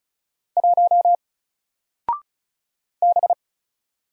Morse code, heard as a keyed tone, about 35 wpm, 700 Hz. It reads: 1 B
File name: Single-Letter-Number-35wpm